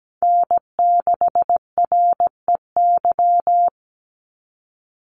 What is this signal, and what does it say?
Morse code, heard as a keyed tone, about 17 wpm, 700 Hz. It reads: N6REY
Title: USA Call Signs 17wpm